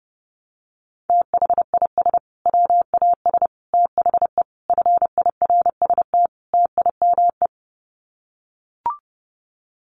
Morse code, heard as a keyed tone, about 30 wpm, 700 Hz. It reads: THIS WAS THE FIRST TIME
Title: Sets of 5 Words Encore 30wpm